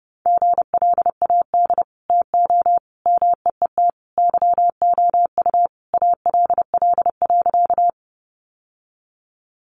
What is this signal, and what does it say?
Morse code, heard as a keyed tone, about 30 wpm, 700 Hz. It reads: GLAD TO MEET YOU ALL.
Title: Sentences from Top 1100 Words 30wpm